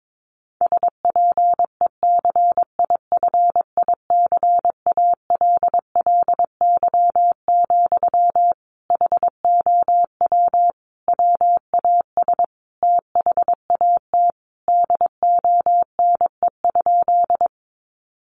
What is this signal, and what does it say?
Morse code, heard as a keyed tone, about 22 wpm, 700 Hz. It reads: SPECIFICALLY, HOW WAS THAT DONE?
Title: Sentences from Top 1100 Words 22wpm